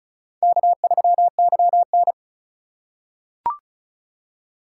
Morse code, heard as a keyed tone, about 34 wpm, 700 Hz. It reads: K3YN
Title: USA Call Signs - 4 Characters - 4x Character Spacing - 35wpm